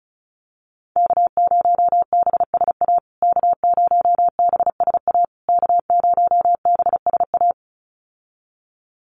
Morse code, heard as a keyed tone, about 34 wpm, 700 Hz. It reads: K0BSA K0BSA K0BSA
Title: USA Call Signs Repeated 3x 35wpm